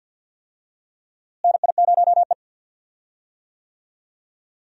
Morse code, heard as a keyed tone, about 50 wpm, 700 Hz. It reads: NI0E
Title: USA Call Signs - 4 Characters - Repeated 2x - 50wpm